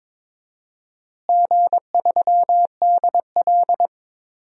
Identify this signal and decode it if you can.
Morse code, heard as a keyed tone, about 22 wpm, 700 Hz. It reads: G3DL
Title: DXCC-100 Call Signs - 4 Characters - Repeated 3x - 22wpm